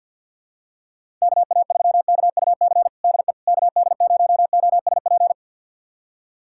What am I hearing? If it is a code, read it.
Morse code, heard as a keyed tone, about 48 wpm, 700 Hz. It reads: KA3KUC DE KD0OSP
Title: USA Callsign DE Callsign 50wpm